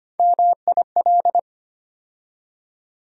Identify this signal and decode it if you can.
Morse code, heard as a keyed tone, about 25 wpm, 700 Hz. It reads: MIL
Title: USA Call Sign Postfixes - 3 Letters 25wpm